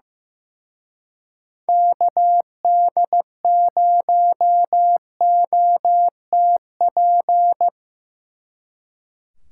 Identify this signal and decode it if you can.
Morse code, heard as a keyed tone, about 17 wpm, 700 Hz. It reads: KD0OTP
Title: Speed Racing - USA 2x3 Call Signs 15wpm